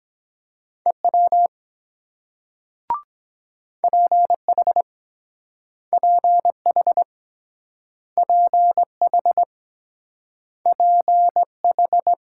Morse code, heard as a keyed tone, about 23 wpm, 690 Hz. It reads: EW PH PH PH PH